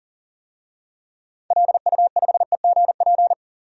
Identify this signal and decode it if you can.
Morse code, heard as a keyed tone, about 40 wpm, 700 Hz. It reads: LU5EGP
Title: DXCC-100 Call Signs - 6 Characters - 40wpm